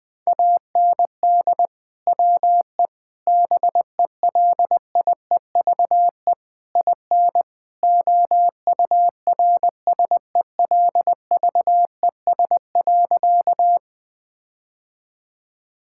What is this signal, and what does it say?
Morse code, heard as a keyed tone, about 20 wpm, 700 Hz. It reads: AND WE BELIEVE IN OURSELVES.
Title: Sentences from Top 1600 Words 20wpm